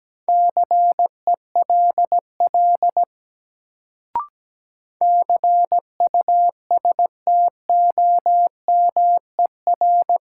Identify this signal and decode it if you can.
Morse code, heard as a keyed tone, about 17 wpm, 700 Hz. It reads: CELL CUSTOMER
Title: Sentences from Top 1100 Words - New Word Drill - 17wpm